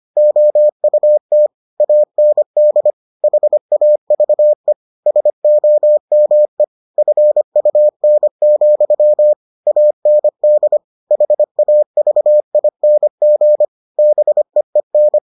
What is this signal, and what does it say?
Morse code, heard as a keyed tone, about 25 wpm, 600 Hz. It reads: OUT AND HAVE SOME FUN, AND HAVING BEEN